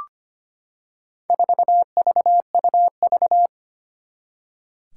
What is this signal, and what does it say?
Morse code, heard as a keyed tone, about 25 wpm, 700 Hz. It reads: 4VUV
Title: Four Challenging Letter-Number Combinations 25wpm